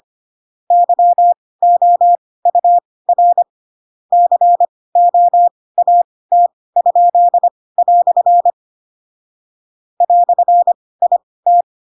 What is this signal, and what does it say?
Morse code, heard as a keyed tone, about 25 wpm, spaced as for 18 wpm, 700 Hz. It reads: YOUR COAT?" "IT